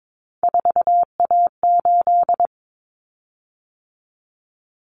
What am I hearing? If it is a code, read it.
Morse code, heard as a keyed tone, about 22 wpm, 700 Hz. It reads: 4A8
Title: Common Cut Numbers - 3-Digit 22wpm